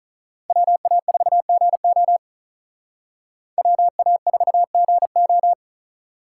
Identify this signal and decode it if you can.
Morse code, heard as a keyed tone, about 45 wpm, 700 Hz. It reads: WA4GO WA4GO